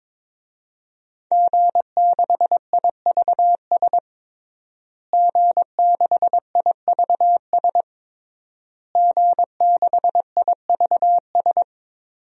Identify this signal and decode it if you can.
Morse code, heard as a keyed tone, about 22 wpm, 700 Hz. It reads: G6IVS G6IVS G6IVS
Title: DXCC-100 Call Signs - 5 Characters - Repeated 3x - 22wpm